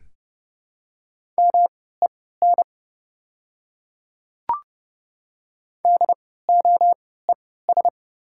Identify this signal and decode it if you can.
Morse code, heard as a keyed tone, about 30 wpm, spaced as for 17 wpm, 690 Hz. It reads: MEN DOES
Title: 3x Character Spacing - Top 200 Words - 30wpm